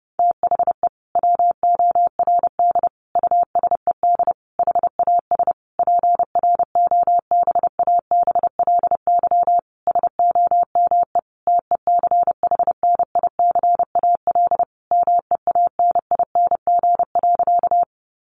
Morse code, heard as a keyed tone, about 30 wpm, 700 Hz. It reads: THE WORD USED HAS PROBABLY SOME TECHNICAL MEANING.
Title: Sentences from Top 1100 Words 30wpm